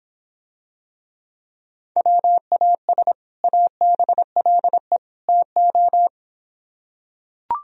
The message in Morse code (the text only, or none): WAS ABLE TO